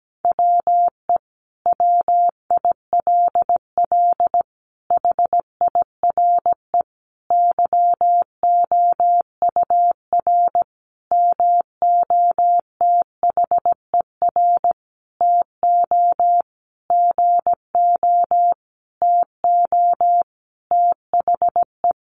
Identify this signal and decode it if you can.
Morse code, heard as a keyed tone, about 17 wpm, 700 Hz. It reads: WE WILL HIRE YOUR MOTHER TO GO TO THE
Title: Sentences from Top 1400 Words 17wpm